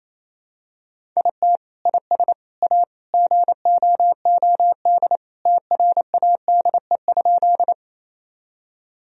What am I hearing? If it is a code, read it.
Morse code, heard as a keyed tone, about 28 wpm, 700 Hz. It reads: IT IS A GOOD TRADE?
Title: Sentences from Top 1000 Words - Review Part 4 - 28wpm